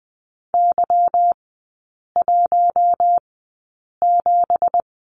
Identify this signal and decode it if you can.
Morse code, heard as a keyed tone, about 20 wpm, spaced as for 8 wpm, 700 Hz. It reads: Y17
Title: 212 Lesson 34e - Number 8 - Three Characters 20wpm